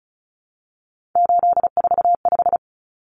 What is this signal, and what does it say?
Morse code, heard as a keyed tone, about 34 wpm, 700 Hz. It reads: 845
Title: Common Cut Numbers - 3-Digit 35wpm